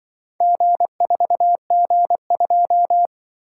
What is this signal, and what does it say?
Morse code, heard as a keyed tone, about 24 wpm, 700 Hz. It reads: G4G2
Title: Speed-Racing - Four Letter-Number Combinations - 20wpm